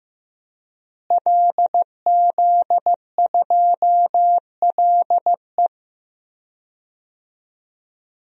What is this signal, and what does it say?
Morse code, heard as a keyed tone, about 15 wpm, 700 Hz. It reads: LZ2LE
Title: DXCC-100 Call Signs - 5 Characters - Repeated 2x - 15wpm